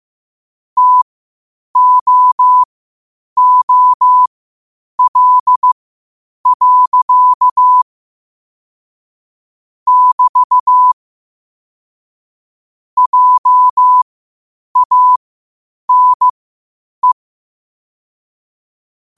Morse code, heard as a keyed tone, about 15 wpm, spaced as for 8 wpm, 1000 Hz. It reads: TOOL. <BT> JANE
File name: 8wpm